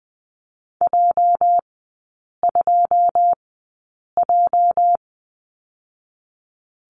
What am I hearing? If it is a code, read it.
Morse code, heard as a keyed tone, about 20 wpm, spaced as for 8 wpm, 690 Hz. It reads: J2J
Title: 205 Lesson 33e - Letter j - Three Characters 20wpm